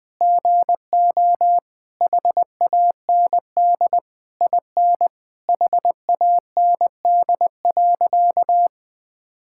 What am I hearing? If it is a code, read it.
Morse code, heard as a keyed tone, about 20 wpm, 700 Hz. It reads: GO HAND IN HAND.
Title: Sentences from Top 1000 Words - Review Part 6 - 20wpm